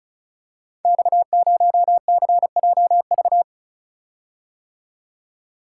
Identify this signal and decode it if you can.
Morse code, heard as a keyed tone, about 36 wpm, 700 Hz. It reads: X0CJV
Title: Code Groups - 5 Letter:Numbers - 35wpm